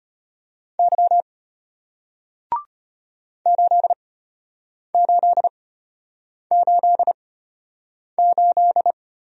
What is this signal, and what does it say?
Morse code, heard as a keyed tone, about 34 wpm, spaced as for 8 wpm, 700 Hz. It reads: Y 8888